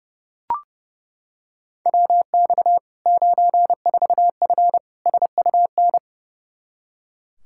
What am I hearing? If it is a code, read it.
Morse code, heard as a keyed tone, about 32 wpm, 700 Hz. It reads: WX 94F SUN
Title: Practice QSO - Exchange 2 - Info - 30wpm